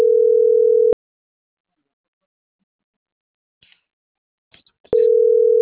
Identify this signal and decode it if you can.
Ringback tone (North America), 440 + 480 Hz.